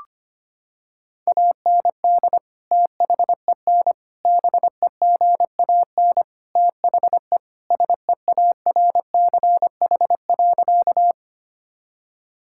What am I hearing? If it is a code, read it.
Morse code, heard as a keyed tone, about 25 wpm, 700 Hz. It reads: AND THEN BEGAN THE SEARCH.
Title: Sentences from Top 1000 Words - Review Part 5 - 25wpm